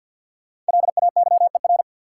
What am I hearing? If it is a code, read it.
Morse code, heard as a keyed tone, about 53 wpm, 700 Hz. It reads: LAYER
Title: Sentences from Top 1600 Words - New Word Drill - 50wpm